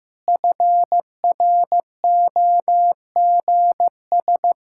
Morse code, heard as a keyed tone, about 15 wpm, 700 Hz. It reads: FROGS
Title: Pangram Challenge v2 - 15wpm